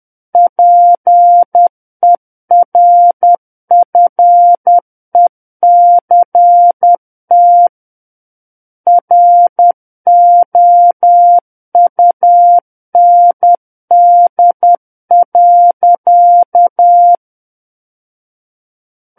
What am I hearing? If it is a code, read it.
Morse code, heard as a keyed tone, about 10 wpm, 700 Hz. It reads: PERFECT ROUND.
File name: Book One / 01-Chapter 1 - 10 WPM